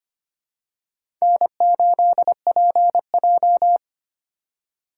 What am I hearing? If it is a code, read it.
Morse code, heard as a keyed tone, about 25 wpm, 700 Hz. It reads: N8PJ
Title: USA Call Signs - 4 Characters - 2x Character Spacing - 25wpm